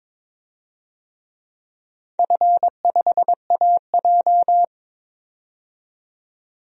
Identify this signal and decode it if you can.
Morse code, heard as a keyed tone, about 22 wpm, 700 Hz. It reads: F5AJ
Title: DXCC-100 Call Signs - 4 Characters - 22wpm